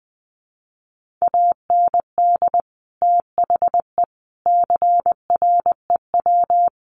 Morse code, heard as a keyed tone, about 20 wpm, 700 Hz. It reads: AND THE CREW
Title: Sentences from Top 1500 Words 20wpm